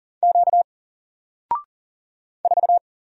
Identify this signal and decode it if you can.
Morse code, heard as a keyed tone, about 40 wpm, 700 Hz. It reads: Q 4
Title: Single Character (Letter:Number:Punctuation:Prosign) Alt 40wpm